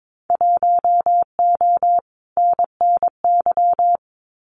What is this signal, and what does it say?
Morse code, heard as a keyed tone, about 22 wpm, 700 Hz. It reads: 1O NNY
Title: Winter Field Day Exchanges 22wpm